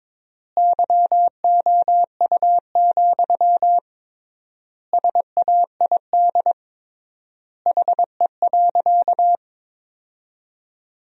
Morse code, heard as a keyed tone, about 22 wpm, 700 Hz. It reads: YOU, SAID HE.
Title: 3x Word-Spacing - Sentences from Top 400 Words - 22wpm